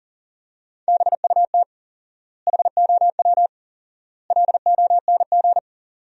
Morse code, heard as a keyed tone, about 40 wpm, 700 Hz. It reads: BUT HOW LONG
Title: 4x Word-Spacing - Sentences from Top 500 Words - 40wpm